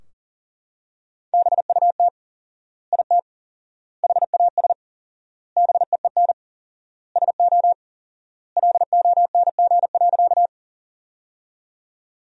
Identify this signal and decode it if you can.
Morse code, heard as a keyed tone, about 40 wpm, 700 Hz. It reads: BUT IT HAS BEEN SO LONG.
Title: 4x Word-Spacing - Sentences from Top 100 Words - 40wpm